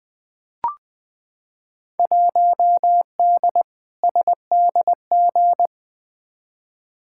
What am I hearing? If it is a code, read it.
Morse code, heard as a keyed tone, about 20 wpm, 700 Hz. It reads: 1D SDG
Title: Field Day Exchanges - 20wpm